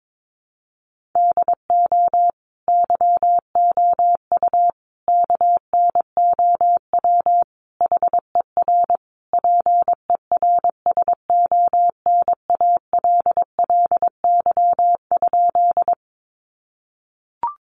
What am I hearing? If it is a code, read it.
Morse code, heard as a keyed tone, about 22 wpm, 700 Hz. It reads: DO YOU KNOW HER PERSONALLY?
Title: Sentences from Top 1400 Words 22wpm